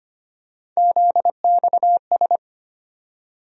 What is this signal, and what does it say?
Morse code, heard as a keyed tone, about 25 wpm, 700 Hz. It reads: ZXS
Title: USA Call Sign Postfixes - 3 Letters 25wpm